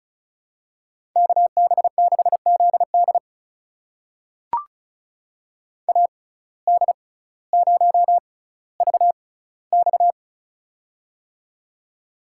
Morse code, heard as a keyed tone, about 34 wpm, spaced as for 18 wpm, 700 Hz. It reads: KB6ZD AD0VX
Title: USA Call Signs - 5 Characters - 6x Character Spacing - 35wpm